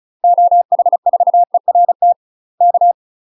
Morse code, heard as a keyed tone, about 35 wpm, 700 Hz. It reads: OH4ERT K
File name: UK MASTER QSOs TRAINING / 35WPM / UK MASTER QSOs 0029